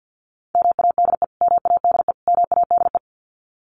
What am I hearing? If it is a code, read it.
Morse code, heard as a keyed tone, about 50 wpm, 700 Hz. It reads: MADE MADE MADE
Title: Top 500 Words Repeated 50wpm